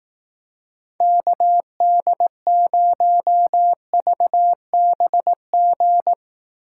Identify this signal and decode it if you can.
Morse code, heard as a keyed tone, about 18 wpm, 700 Hz. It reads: KD0VBG
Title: Speed Racing - USA 2x3 Call Signs 15wpm